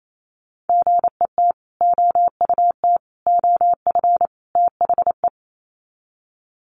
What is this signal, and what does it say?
Morse code, heard as a keyed tone, about 28 wpm, 700 Hz. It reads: GET OUT OF THE
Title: Sets of 4 Words 28wpm